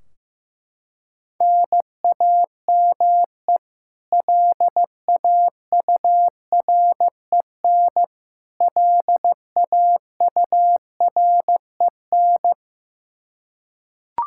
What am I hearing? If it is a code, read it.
Morse code, heard as a keyed tone, about 15 wpm, 700 Hz. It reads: NAME LAUREN LAUREN
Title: Practice QSO - Exchange 1 - Beginning - 15wpm